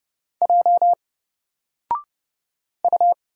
Morse code, heard as a keyed tone, about 29 wpm, 700 Hz. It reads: J U
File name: Single-Letter-Number-30wpm